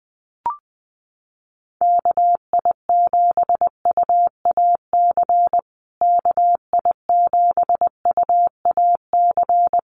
Morse code, heard as a keyed tone, about 20 wpm, 700 Hz. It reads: KI7UAC KI7UAC
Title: USA Call Signs Repeated 3x 20wpm